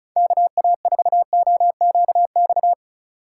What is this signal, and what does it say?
Morse code, heard as a keyed tone, about 36 wpm, 700 Hz. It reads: KA4OQX
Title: USA Call Signs - 6 Characters - 3x Character Spacing - 35wpm